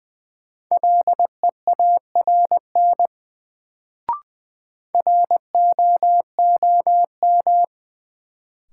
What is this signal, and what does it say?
Morse code, heard as a keyed tone, about 20 wpm, 700 Hz. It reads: LEARN ROOM
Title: Sentences from Top 300 Words - New Word Drill - 20wpm